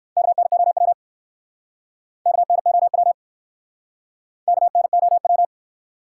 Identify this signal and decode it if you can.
Morse code, heard as a keyed tone, about 60 wpm, 700 Hz. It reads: KNOW KNOW KNOW